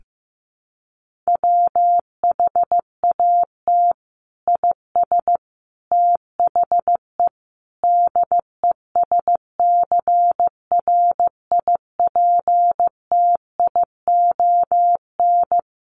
Morse code, heard as a keyed tone, about 15 wpm, 700 Hz. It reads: WHAT IS THE DESCRIPTION
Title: Sentences from Top 1300 Words 15wpm